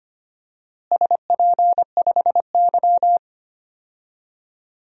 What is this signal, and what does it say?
Morse code, heard as a keyed tone, about 25 wpm, 700 Hz. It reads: SP5Y
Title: DXCC-100 Call Signs - 4 Characters - Repeated 2x - 25wpm